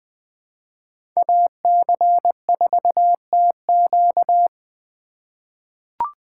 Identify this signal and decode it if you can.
Morse code, heard as a keyed tone, about 20 wpm, 700 Hz. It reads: AC4TQ
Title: USA Call Signs - 5 Characters - 2x Character Spacing - 20wpm